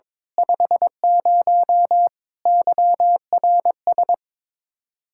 Morse code, heard as a keyed tone, about 22 wpm, 700 Hz. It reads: 50 YRS
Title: Practice QSO - Exchange 3 - More Info - 22wpm